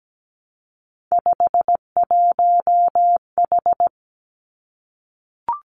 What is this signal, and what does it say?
Morse code, heard as a keyed tone, about 17 wpm, 700 Hz. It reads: 51H
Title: Three Challenging Letter-Number Combinations 17wpm